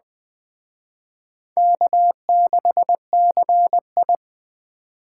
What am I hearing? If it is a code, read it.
Morse code, heard as a keyed tone, about 21 wpm, 700 Hz. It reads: K6CI